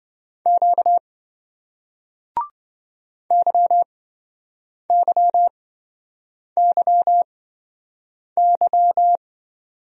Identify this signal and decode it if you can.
Morse code, heard as a keyed tone, about 27 wpm, spaced as for 7 wpm, 700 Hz. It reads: Q YYYY